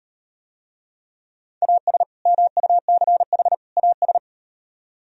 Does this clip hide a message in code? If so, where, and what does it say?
Yes: Morse code from 1.6 to 4.2 s, heard as a keyed tone, about 38 wpm, 700 Hz. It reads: AS MUCH AS